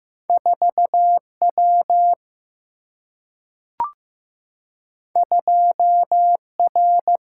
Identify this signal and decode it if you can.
Morse code, heard as a keyed tone, about 15 wpm, 700 Hz. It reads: 4W 2R
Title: Single-Letter Single-Number 15wpm